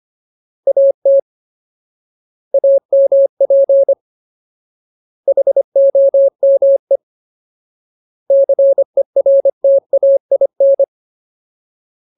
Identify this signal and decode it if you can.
Morse code, heard as a keyed tone, about 25 wpm, 550 Hz. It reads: AT AMP HOME CERTAIN
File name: Ultimate Word List / 25WPM Ultimate